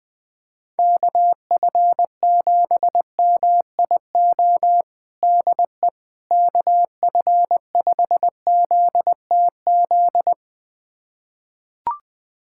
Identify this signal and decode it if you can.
Morse code, heard as a keyed tone, about 20 wpm, 700 Hz. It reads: KF7MIO DE KF5ZTZ
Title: USA Callsign DE Callsign 20wpm